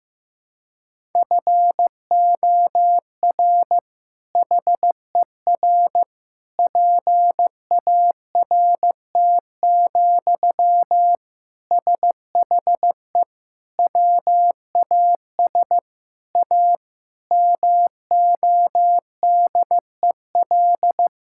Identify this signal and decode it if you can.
Morse code, heard as a keyed tone, about 16 wpm, 700 Hz. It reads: FOR HER PART, SHE WAS A MODEL
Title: Sentences from Top 1000 Words - Review Part 5 - 15wpm